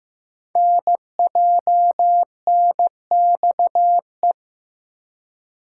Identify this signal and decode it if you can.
Morse code, heard as a keyed tone, about 15 wpm, 700 Hz. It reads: NJNXE
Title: Code Groups - 5 Letters - 15wpm